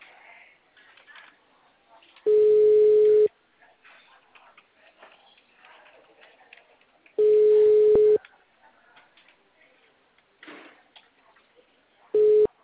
Ringback tone (Europe, 425 Hz).